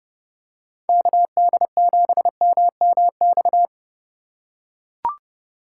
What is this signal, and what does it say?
Morse code, heard as a keyed tone, about 30 wpm, 700 Hz. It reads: KD7MMX
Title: USA Call Signs - 6 Characters - 2x Character Spacing - 30wpm